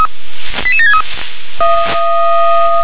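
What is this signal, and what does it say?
Five-tone selective call (ZVEI), heard as quick runs of five tones, tones of about 80 ms. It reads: 88763
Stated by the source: ZVEI five-tone selective call received on SDR.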